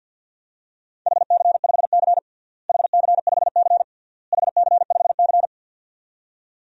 Morse code, heard as a keyed tone, about 47 wpm, 700 Hz. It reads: HC5C HC5C HC5C
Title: DXCC-100 Call Signs - 4 Characters - Repeated 3x - 50wpm